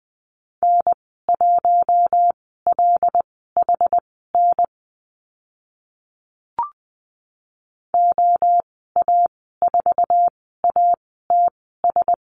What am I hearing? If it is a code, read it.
Morse code, heard as a keyed tone, about 20 wpm, spaced as for 14 wpm, 700 Hz. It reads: N1LHN OA4ATS
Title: 082 Lesson 14g - Number 2 - Call Signs 20wpm